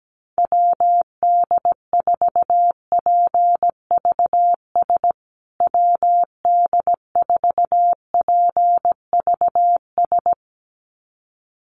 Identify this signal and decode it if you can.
Morse code, heard as a keyed tone, about 17 wpm, 700 Hz. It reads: WD4PVS WD4PVS
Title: USA Call Signs Repeated 2x 17wpm